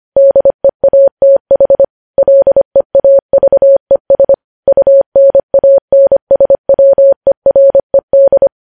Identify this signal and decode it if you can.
Morse code, heard as a keyed tone, about 25 wpm, 550 Hz. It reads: DEATH LEAVES UNANSWERED
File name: podcast / News headlines-1x-25x25